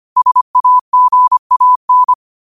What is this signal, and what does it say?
Morse code, heard as a keyed tone, about 25 wpm, 1000 Hz. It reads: SAGAN
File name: qotdfiles / 25wpm